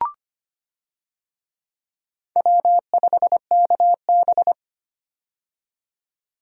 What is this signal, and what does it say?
Morse code, heard as a keyed tone, about 25 wpm, 700 Hz. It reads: W5KB
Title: USA Call Signs - 4 Characters - 25wpm